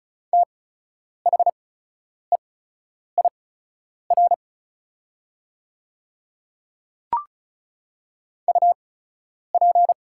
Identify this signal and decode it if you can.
Morse code, heard as a keyed tone, about 36 wpm, spaced as for 10 wpm, 700 Hz. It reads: THEIR UP
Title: 8x Character Spacing - Top 100 Words - 35wpm